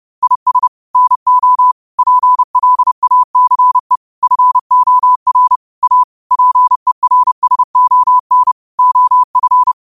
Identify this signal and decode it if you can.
Morse code, heard as a keyed tone, about 30 wpm, 1000 Hz. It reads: IS NO PLACE FOR A PERSON OF